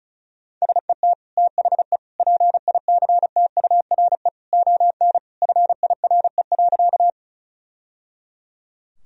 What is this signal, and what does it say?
Morse code, heard as a keyed tone, about 36 wpm, 700 Hz. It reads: SET THE PICTURE ON FIRE.
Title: Sentences from Top 1000 Words - Review Part 1 - 35wpm